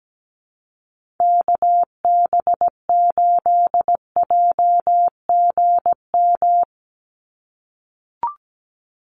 Morse code, heard as a keyed tone, about 17 wpm, 700 Hz. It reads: KB8JGM
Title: USA Call Signs 17wpm